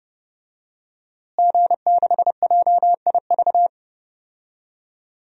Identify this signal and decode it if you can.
Morse code, heard as a keyed tone, about 30 wpm, 700 Hz. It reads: G6JIV
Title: DXCC-100 Call Signs - 5 Characters - Repeated 3x - 30wpm